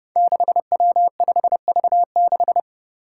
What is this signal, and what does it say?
Morse code, heard as a keyed tone, about 30 wpm, 700 Hz. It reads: 6W5V6
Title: Five Challenging Letter-Number Combinations 30wpm